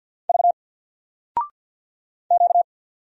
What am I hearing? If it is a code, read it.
Morse code, heard as a keyed tone, about 50 wpm, 700 Hz. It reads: V Q
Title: Single Letter-Number 50wpm